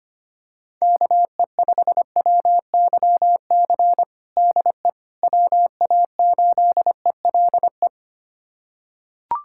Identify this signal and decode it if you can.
Morse code, heard as a keyed tone, about 25 wpm, 700 Hz. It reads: KE5WYC DE WA8ELE
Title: USA Callsign DE Callsign - 6 Characters - 25wpm